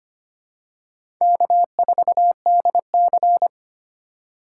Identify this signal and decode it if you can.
Morse code, heard as a keyed tone, about 25 wpm, 700 Hz. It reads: K4DC
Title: USA Call Signs - 4 Characters - 3x Character Spacing - 25wpm